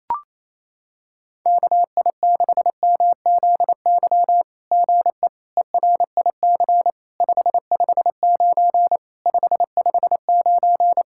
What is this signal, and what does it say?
Morse code, heard as a keyed tone, about 29 wpm, 700 Hz. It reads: KI6MZY GE ERIC 559 559
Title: SOTA QSO Practice 28wpm